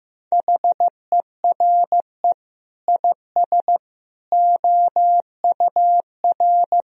Morse code, heard as a keyed tone, about 15 wpm, 700 Hz. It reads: HERE IS OUR
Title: Sentences from Top 1000 Words - Review Part 2 - 15wpm